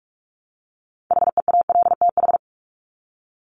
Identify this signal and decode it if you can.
Morse code, heard as a keyed tone, about 44 wpm, 700 Hz. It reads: HEALTH
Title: Top 501-750 Words 45wpm